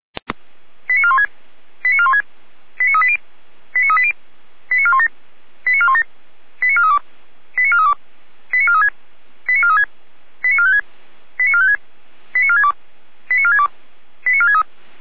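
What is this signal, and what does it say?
Five-tone selective call (ZVEI), heard as quick runs of five tones, tones of about 70 ms. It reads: 89416 89416 89380 89380 89417 89417 89432 89432 89436 89436 89456 89456 89462 89462 89463
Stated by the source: Received on SDR; ZVEI five-tone call.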